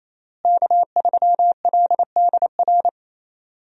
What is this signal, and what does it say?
Morse code, heard as a keyed tone, about 28 wpm, 700 Hz. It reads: K3LDR
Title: USA Call Signs - 5 Characters - Repeated 3x - 28wpm